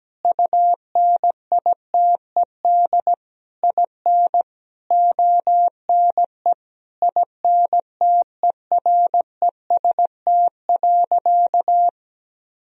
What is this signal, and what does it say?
Morse code, heard as a keyed tone, about 17 wpm, 700 Hz. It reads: UNITED IN ONE INTEREST.
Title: Sentences from Top 1000 Words - Review Part 4 - 17wpm